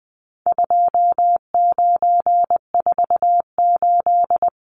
Morse code, heard as a keyed tone, about 20 wpm, 700 Hz. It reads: 2948
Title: Numbers - 4-Digit 20wpm